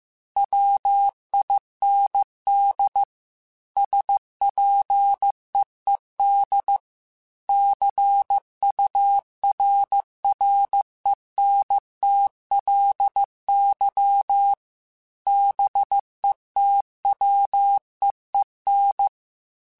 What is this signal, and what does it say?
Morse code, heard as a keyed tone, about 15 wpm, 800 Hz. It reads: WIND SPEED CURRENTLY BETWEEN
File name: Weather Sets / Weather Sets-mp3s / Weather 5 22 @13